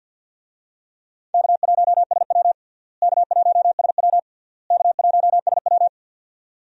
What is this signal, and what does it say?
Morse code, heard as a keyed tone, about 50 wpm, 700 Hz. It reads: K1SW K1SW K1SW
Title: USA Call Signs - 4 Characters - Repeated 3x - 50wpm